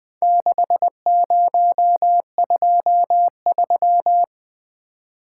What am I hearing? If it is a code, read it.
Morse code, heard as a keyed tone, about 20 wpm, 700 Hz. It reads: 6023
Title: Four Letter-Number Combinations - 20wpm